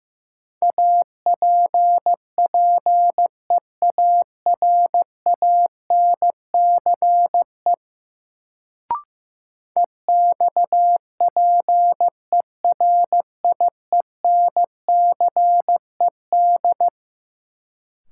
Morse code, heard as a keyed tone, about 15 wpm, 700 Hz. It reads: APPEARANCE EXPERIENCED
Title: Sentences from Top 1100 Words - New Word Drill Alt - 15wpm